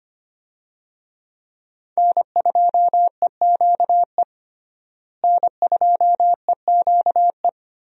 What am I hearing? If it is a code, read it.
Morse code, heard as a keyed tone, about 25 wpm, 700 Hz. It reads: N2EQE N2EQE
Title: USA Call Signs - 5 Characters - Repeated 2x - 25wpm